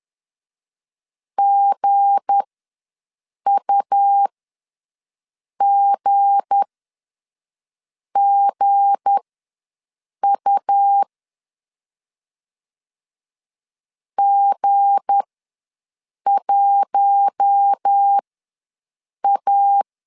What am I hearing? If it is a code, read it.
Morse code, heard as a keyed tone, about 10 wpm, spaced as for 5 wpm, 800 Hz. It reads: GUGGU G1A
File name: LEZIONI / 7 GF9Z / 7 G ws 10wpm 3843